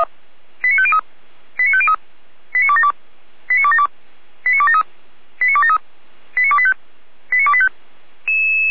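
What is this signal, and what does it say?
Five-tone selective call (ZVEI), heard as quick runs of five tones, tones of about 70 ms. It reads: 89503 89503 89272 89272 89273 89273 89285 89285
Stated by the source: Received by SDR, ZVEI five-tone call 89208.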